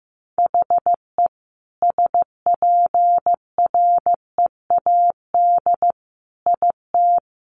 Morse code, heard as a keyed tone, about 15 wpm, 700 Hz. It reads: HE SPREAD IT
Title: Sentences from Top 1200 Words 15wpm